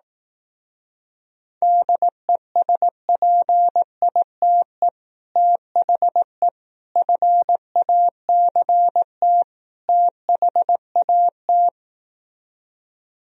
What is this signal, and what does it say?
Morse code, heard as a keyed tone, about 18 wpm, 700 Hz. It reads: DESPITE THE FACT THAT
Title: Speed-Racing - Sets of 4 Words 15wpm